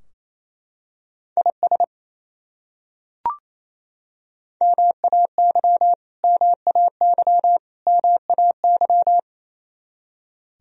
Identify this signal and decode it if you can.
Morse code, heard as a keyed tone, about 28 wpm, 700 Hz. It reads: IS MAY MAY MAY
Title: Top 100 Words Repeated 28wpm